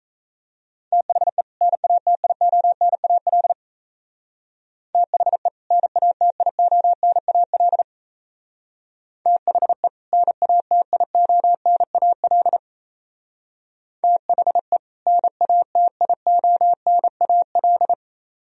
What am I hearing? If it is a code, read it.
Morse code, heard as a keyed tone, about 36 wpm, 700 Hz. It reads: THE NATIONAL THE NATIONAL THE NATIONAL THE NATIONAL